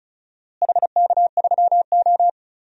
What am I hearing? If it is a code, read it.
Morse code, heard as a keyed tone, about 34 wpm, 700 Hz. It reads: HK3O
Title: DXCC-100 Call Signs - 4 Characters - Repeated 2x - 35wpm